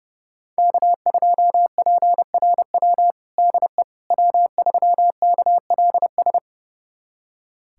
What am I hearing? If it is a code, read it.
Morse code, heard as a keyed tone, about 32 wpm, 700 Hz. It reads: K2PRW DE W3KLS
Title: USA Callsign DE Callsign - 5 Characters - 30wpm